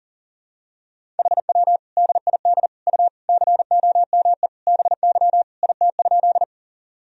Morse code, heard as a keyed tone, about 40 wpm, 700 Hz. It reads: HW DID U COME BY IT?
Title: Sentences from Top 100 Words and Abbreviations 40wpm